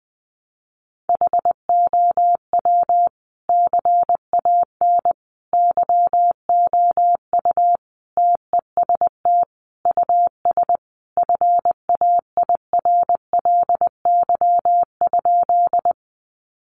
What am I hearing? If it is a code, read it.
Morse code, heard as a keyed tone, about 20 wpm, 700 Hz. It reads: HOW CAN YOU TEST US FAIRLY?
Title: Sentences from Top 1100 Words 20wpm